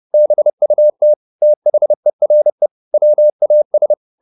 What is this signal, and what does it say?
Morse code, heard as a keyed tone, about 30 wpm, 600 Hz. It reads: BUT THERE WAS